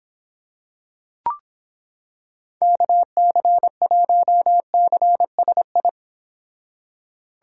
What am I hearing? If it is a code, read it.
Morse code, heard as a keyed tone, about 26 wpm, 700 Hz. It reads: KC1CSI